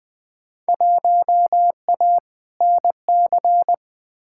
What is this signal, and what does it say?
Morse code, heard as a keyed tone, about 20 wpm, 700 Hz. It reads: 1A NC
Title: Field Day Exchanges Encore - 20wpm